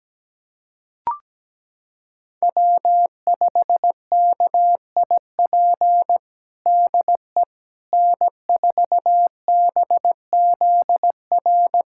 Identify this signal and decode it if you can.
Morse code, heard as a keyed tone, about 17 wpm, 700 Hz. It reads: W5KIP DE N4BZR
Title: USA Callsign DE Callsign - 5 Characters - 17wpm